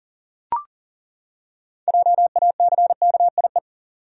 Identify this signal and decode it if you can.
Morse code, heard as a keyed tone, about 40 wpm, 700 Hz. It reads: JACKIE
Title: USA Common Names - 40wpm